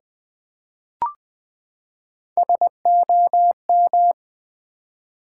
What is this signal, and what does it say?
Morse code, heard as a keyed tone, about 20 wpm, 700 Hz. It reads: SOM
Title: Build A Word - 3 Letters At A Time - 9-Letter Words - 20wpm